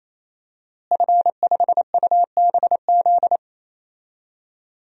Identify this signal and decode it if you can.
Morse code, heard as a keyed tone, about 27 wpm, 700 Hz. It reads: F5UBZ
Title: DXCC-100 Call Signs - 5 Characters - Repeated 3x - 28wpm